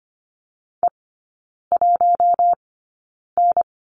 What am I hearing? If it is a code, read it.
Morse code, heard as a keyed tone, about 25 wpm, spaced as for 9 wpm, 690 Hz. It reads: E1N
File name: Lesson 15e - Number 5 - Three Characters 25wpm